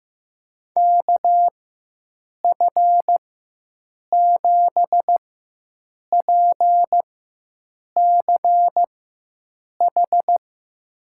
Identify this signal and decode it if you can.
Morse code, heard as a keyed tone, about 15 wpm, spaced as for 7 wpm, 700 Hz. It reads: KF7PCH
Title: USA Call Signs - 4x Character Spacing - 15wpm